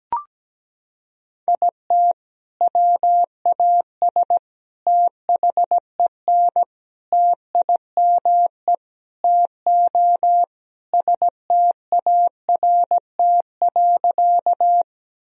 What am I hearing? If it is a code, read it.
Morse code, heard as a keyed tone, about 17 wpm, 700 Hz. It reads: IT WAS THEN TIME TO START.
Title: Sentences from Top 1000 Words - Review Part 1 - 17wpm